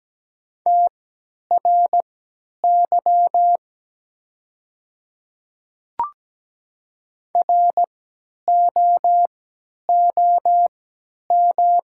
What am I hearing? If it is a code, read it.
Morse code, heard as a keyed tone, about 17 wpm, spaced as for 10 wpm, 700 Hz. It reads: TRY ROOM
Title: 3x Character Spacing - Top 500 Words - 17wpm